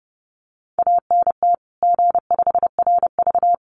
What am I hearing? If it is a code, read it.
Morse code, heard as a keyed tone, about 30 wpm, 700 Hz. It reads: ANT G5RV
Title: Common QSO Phrases 30wpm